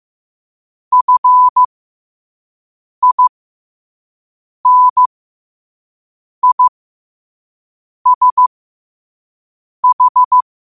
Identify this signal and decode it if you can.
Morse code, heard as a keyed tone, about 15 wpm, spaced as for 5 wpm, 1000 Hz. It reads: FINISH
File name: qotdfiles / 5wpm